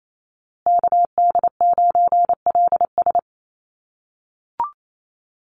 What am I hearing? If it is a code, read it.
Morse code, heard as a keyed tone, about 28 wpm, 700 Hz. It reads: KD9LS
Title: USA Call Signs Repeated 2x 28wpm